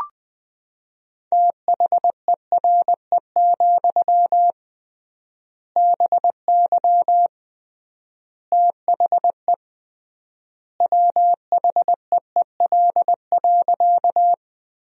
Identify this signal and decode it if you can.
Morse code, heard as a keyed tone, about 21 wpm, 700 Hz. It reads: THERE, BY THE WHEEL.
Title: 3x Word-Spacing - Sentences from Top 500 Words - 20wpm